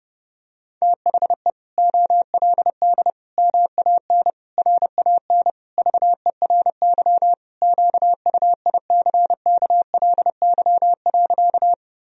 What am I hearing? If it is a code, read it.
Morse code, heard as a keyed tone, about 29 wpm, 700 Hz. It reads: THE OLD MAN RAN VERY QUICKLY.
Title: Sentences from Top 1000 Words - Review Part 3 - 30wpm